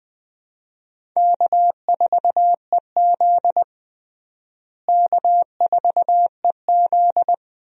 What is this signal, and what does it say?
Morse code, heard as a keyed tone, about 20 wpm, 700 Hz. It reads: K4EZ K4EZ
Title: USA Call Signs - 4 Characters - Repeated 3x - 20wpm